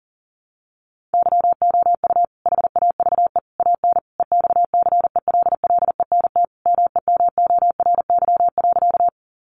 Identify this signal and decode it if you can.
Morse code, heard as a keyed tone, about 37 wpm, 700 Hz. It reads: YOU HAVE AN EXCELLENT MEMORY.
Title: Sentences from Top 1300 Words 40wpm